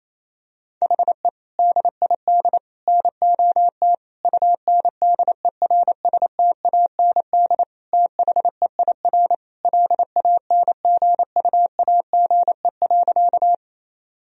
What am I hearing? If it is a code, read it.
Morse code, heard as a keyed tone, about 28 wpm, 700 Hz. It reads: HE DID NOT UNDERSTAND THEIR LANGUAGE.
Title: Sentences from Top 1000 Words - Review Part 3 - 28wpm